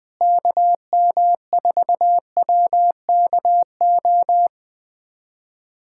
Morse code, heard as a keyed tone, about 20 wpm, 700 Hz. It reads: KM4WKO
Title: USA Callsign DE Callsign 20wpm